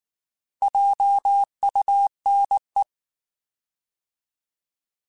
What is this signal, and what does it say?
Morse code, heard as a keyed tone, about 19 wpm, 800 Hz. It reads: JUNE
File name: Alphabetical / Alphabetical mp3s / Js @18